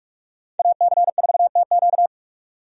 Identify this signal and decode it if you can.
Morse code, heard as a keyed tone, about 47 wpm, 700 Hz. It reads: AC4TQ
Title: USA Call Signs - 5 Characters - 45wpm